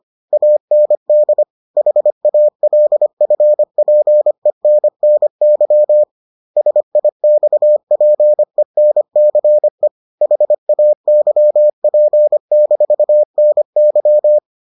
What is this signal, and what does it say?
Morse code, heard as a keyed tone, about 25 wpm, 600 Hz. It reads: AND HALFPENNY SIXPENCE HAYP-NY